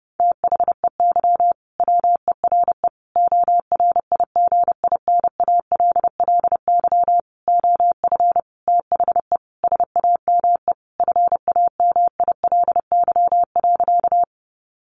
Morse code, heard as a keyed tone, about 30 wpm, 700 Hz. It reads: THEY WERE ORIGINALLY OF THE SAME FAMILY.
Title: Sentences from Top 1500 Words 30wpm